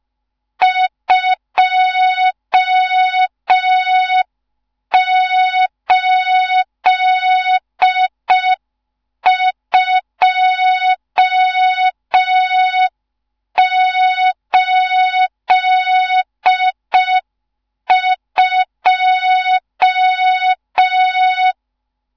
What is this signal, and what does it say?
Morse code, heard as a keyed tone, about 5 wpm, 3700 Hz. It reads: 28282
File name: Morze-HA2MN / hang / 14 280 HA2MN